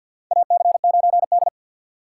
Morse code, heard as a keyed tone, about 51 wpm, 700 Hz. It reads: AC9D
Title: USA Call Signs - 4 Characters - 3x Character Spacing - 50wpm